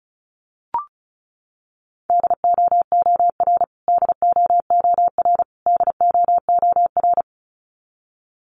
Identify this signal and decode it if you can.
Morse code, heard as a keyed tone, about 34 wpm, 700 Hz. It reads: DOOR DOOR DOOR
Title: Top 300 Words Repeated 35wpm